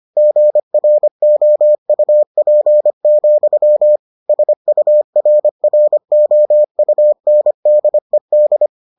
Morse code, heard as a keyed tone, about 25 wpm, 600 Hz. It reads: GROUP, SURROUNDED